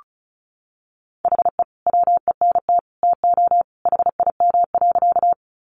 Morse code, heard as a keyed tone, about 34 wpm, 700 Hz. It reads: HE WENT TO HIM.
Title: Sentences from Top 200 Words 35wpm